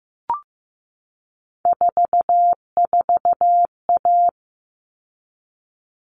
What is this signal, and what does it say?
Morse code, heard as a keyed tone, about 15 wpm, 700 Hz. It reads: 44A
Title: All Cut Numbers - 3-Digit 15wpm